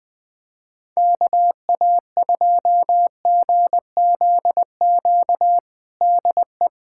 Morse code, heard as a keyed tone, about 21 wpm, 700 Hz. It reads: KA2GZQ DE
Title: USA Callsign DE Callsign 20wpm